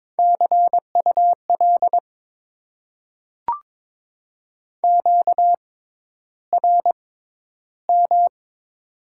Morse code, heard as a keyed tone, about 22 wpm, spaced as for 11 wpm, 700 Hz. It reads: CUL QRM
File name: Top 90 QSO Elements - 6x Character Spacing - 22wpm